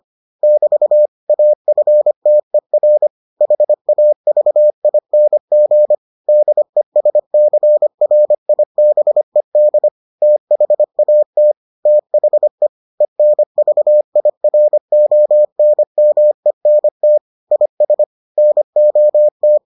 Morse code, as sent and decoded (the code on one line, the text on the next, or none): -...- / .- ..-. - . .-. / .... .- ...- .. -. --. / -.. ... -.-. .-. .. -... -.. / - .... .- - / - .... / . -. ...- .. .-. --- -. -- . -. - / .. ... / -. --- -
<BT> AFTER HAVING DESCRIBED THAT THE ENVIRONMENT IS NOT